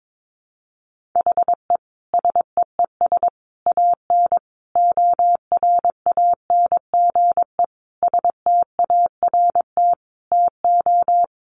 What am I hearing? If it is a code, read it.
Morse code, heard as a keyed tone, about 22 wpm, 700 Hz. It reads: HE SEES AN ORANGE START TO
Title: Easy Phrases 22wpm